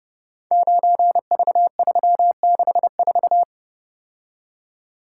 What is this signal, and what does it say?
Morse code, heard as a keyed tone, about 30 wpm, 700 Hz. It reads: 9V364
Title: Code Groups - 5 Letter:Numbers - 30wpm